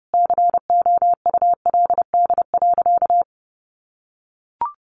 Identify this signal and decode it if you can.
Morse code, heard as a keyed tone, about 28 wpm, 700 Hz. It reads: COULD.
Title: Sentences from Top 500 Words 30wpm